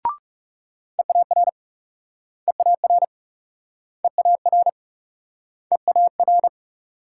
Morse code, heard as a keyed tone, about 37 wpm, 700 Hz. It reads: EAR EAR EAR EAR
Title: Speed-Racing Multi-letter Phonemes - 30wpm